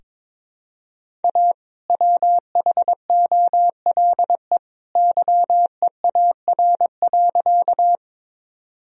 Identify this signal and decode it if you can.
Morse code, heard as a keyed tone, about 22 wpm, 700 Hz. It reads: A WHOLE YEAR.
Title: Sentences from Top 1000 Words - Review Part 1 - 22wpm